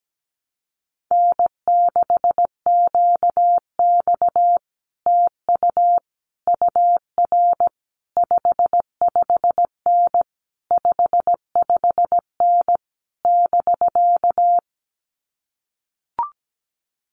Morse code, heard as a keyed tone, about 17 wpm, 700 Hz. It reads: N6QX TU UR 55N 55N <BK>
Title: Common QSO Phrases Encore 17wpm